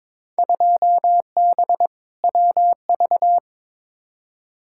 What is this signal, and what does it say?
Morse code, heard as a keyed tone, about 22 wpm, 700 Hz. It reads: 2B WV
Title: Field Day Exchanges Encore - 22wpm